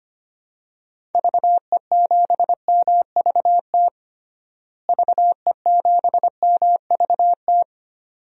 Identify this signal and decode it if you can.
Morse code, heard as a keyed tone, about 25 wpm, 700 Hz. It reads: VE7MVT VE7MVT
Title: DXCC-100 Call Signs - 6 Characters - Repeated 2x - 25wpm